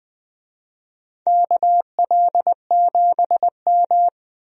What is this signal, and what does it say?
Morse code, heard as a keyed tone, about 20 wpm, 700 Hz. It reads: KL7M
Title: DXCC-100 Call Signs - 4 Characters - 20wpm